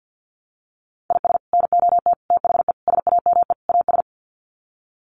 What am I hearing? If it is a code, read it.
Morse code, heard as a keyed tone, about 50 wpm, 700 Hz. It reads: IS NOT THE SAME AS
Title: Sets of 5 Words Encore 50wpm